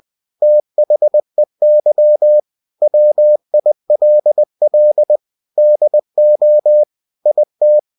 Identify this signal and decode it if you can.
Morse code, heard as a keyed tone, about 20 wpm, 600 Hz. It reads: THEY WILL DO IT
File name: cw-books / winnie-the-pooh / winnie-20wpm-page-0093